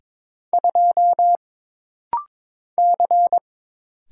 Morse code, heard as a keyed tone, about 22 wpm, 700 Hz. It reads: 2 C
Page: This rapid-fire timing is geared to push your ICR (Instant Character Recognition) skills.